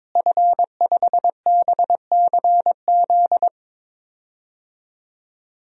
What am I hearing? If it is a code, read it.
Morse code, heard as a keyed tone, about 22 wpm, 700 Hz. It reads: F5BCZ
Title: DXCC-100 Call Signs - 5 Characters - Repeated 3x - 22wpm